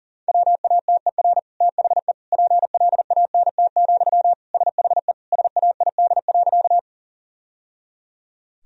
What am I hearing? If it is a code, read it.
Morse code, heard as a keyed tone, about 40 wpm, 700 Hz. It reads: WATER THE PLANT, SHE SAID.
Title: Sentences from Top 1000 Words - Review Part 1 - 40wpm